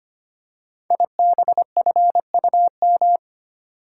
Morse code, heard as a keyed tone, about 25 wpm, 700 Hz. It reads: IBFUM
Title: Code Groups - 5 Letter:Numbers - 25wpm